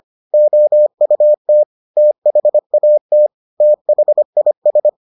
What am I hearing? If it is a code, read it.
Morse code, heard as a keyed tone, about 25 wpm, 600 Hz. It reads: OUT THAT THIS